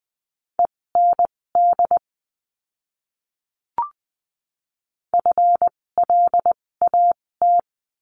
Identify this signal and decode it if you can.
Morse code, heard as a keyed tone, about 20 wpm, spaced as for 16 wpm, 700 Hz. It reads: END FLAT
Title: 256 Lesson 42a - Words 12-20wpm